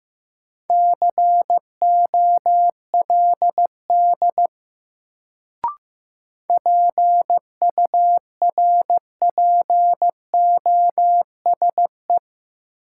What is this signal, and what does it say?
Morse code, heard as a keyed tone, about 15 wpm, 700 Hz. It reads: COLD PURPOSE
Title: Sets of 5 Words - New Word Drill - 15wpm